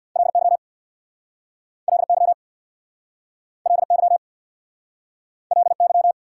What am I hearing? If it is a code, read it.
Morse code, heard as a keyed tone, about 69 wpm, 700 Hz. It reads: LY LY LY LY